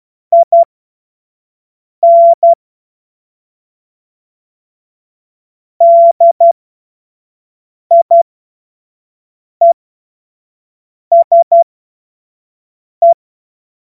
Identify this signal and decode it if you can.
Morse code, heard as a keyed tone, about 12 wpm, spaced as for 5 wpm, 680 Hz. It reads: IN DIESE
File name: Morsekurs / Witze / Witz 07. Juni 2022 25 F70